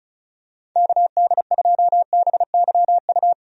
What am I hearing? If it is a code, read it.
Morse code, heard as a keyed tone, about 36 wpm, 700 Hz. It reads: KD2BYU
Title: USA Call Signs - 6 Characters - 35wpm